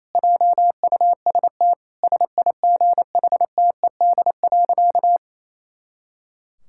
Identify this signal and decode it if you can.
Morse code, heard as a keyed tone, about 27 wpm, 700 Hz. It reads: JUST SIGHTED.
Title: Pangram Challenge v2 - 28wpm